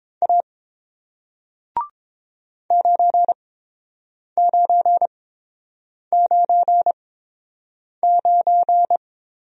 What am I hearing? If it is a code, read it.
Morse code, heard as a keyed tone, about 30 wpm, spaced as for 7 wpm, 700 Hz. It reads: A 9999